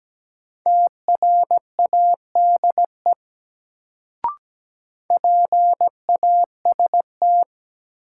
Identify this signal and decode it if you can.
Morse code, heard as a keyed tone, about 17 wpm, 700 Hz. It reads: TRADE PAST
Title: Sentences from Top 600 Words - New Word Drill - 17wpm